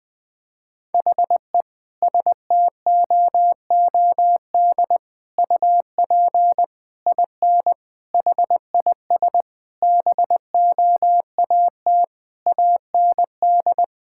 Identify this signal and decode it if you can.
Morse code, heard as a keyed tone, about 20 wpm, 700 Hz. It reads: HE STOOD UP IN HIS BOAT AND
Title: Sentences from Top 1600 Words 20wpm